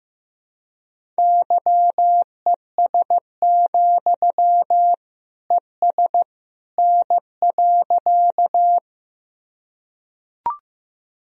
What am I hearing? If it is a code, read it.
Morse code, heard as a keyed tone, about 15 wpm, 700 Hz. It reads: YES, ES N.
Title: Sentences from Top 500 Words and Abbreviations 15wpm